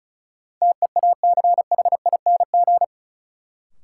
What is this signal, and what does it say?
Morse code, heard as a keyed tone, about 36 wpm, 700 Hz. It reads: TEACHING
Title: Sentences from Top 1100 Words - New Word Drill Alt - 35wpm